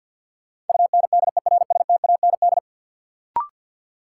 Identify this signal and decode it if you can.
Morse code, heard as a keyed tone, about 48 wpm, 700 Hz. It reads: UNDERSTAND
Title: Sets of 4 Words - New Word Drill - 50wpm